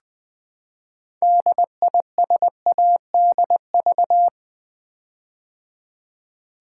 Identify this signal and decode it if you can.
Morse code, heard as a keyed tone, about 20 wpm, 700 Hz. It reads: DISADV
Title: Build A Word - 6 Letters At A Time - 12-Letter Words - 20wpm